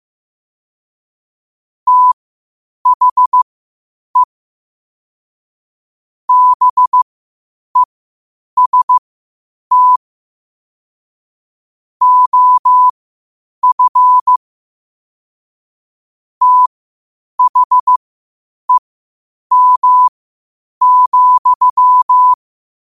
Morse code, heard as a keyed tone, about 15 wpm, spaced as for 8 wpm, 1000 Hz. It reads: THE BEST OF THEM,
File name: qotdfiles / 8wpm